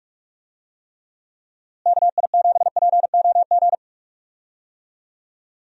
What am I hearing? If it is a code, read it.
Morse code, heard as a keyed tone, about 46 wpm, 700 Hz. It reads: KI7POG
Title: USA Call Signs - 6 Characters - Repeated 3x - 45wpm